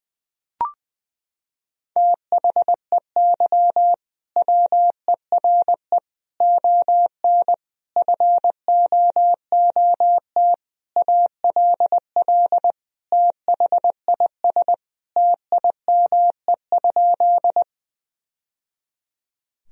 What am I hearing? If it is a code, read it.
Morse code, heard as a keyed tone, about 21 wpm, 700 Hz. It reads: THEY WERE ON FOOT ALL THIS TIME?
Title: Sentences from Top 1000 Words - Review Part 3 - 20wpm